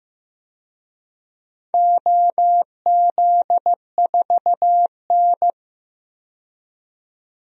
Morse code, heard as a keyed tone, about 15 wpm, 700 Hz. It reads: OZ4N
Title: DXCC-100 Call Signs - 4 Characters - 15wpm